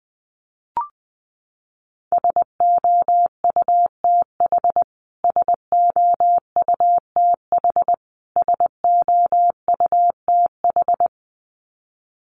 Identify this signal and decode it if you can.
Morse code, heard as a keyed tone, about 20 wpm, 700 Hz. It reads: SOUTH SOUTH SOUTH
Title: Top 500 Words Repeated 20wpm